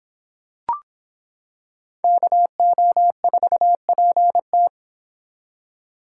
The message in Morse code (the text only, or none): KO4PT